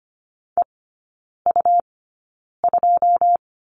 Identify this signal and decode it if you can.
Morse code, heard as a keyed tone, about 25 wpm, spaced as for 9 wpm, 700 Hz. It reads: EU2
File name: Lesson 17e - Letter u - Three Characters 25wpm